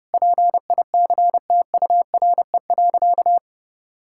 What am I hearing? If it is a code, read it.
Morse code, heard as a keyed tone, about 31 wpm, 700 Hz. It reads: PICTURE.
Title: Sentences from Top 1000 Words - Review Part 6 - 30wpm